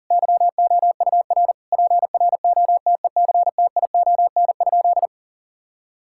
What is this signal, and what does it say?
Morse code, heard as a keyed tone, about 40 wpm, 700 Hz. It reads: YOUR PROTECTION?
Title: Sentences from Top 1600 Words 40wpm